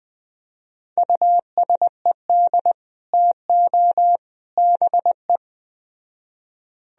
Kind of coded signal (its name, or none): Morse code